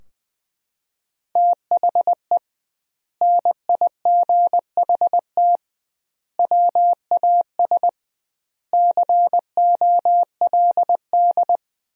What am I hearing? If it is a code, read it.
Morse code, heard as a keyed tone, about 20 wpm, 700 Hz. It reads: THE NIGHT WAS COLD
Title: 2x Word-Spacing - Sentences from Top 400 Words - 20wpm